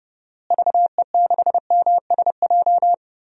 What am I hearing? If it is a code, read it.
Morse code, heard as a keyed tone, about 30 wpm, 700 Hz. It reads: VE6MSJ
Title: DXCC-100 Call Signs - 6 Characters - Repeated 3x - 30wpm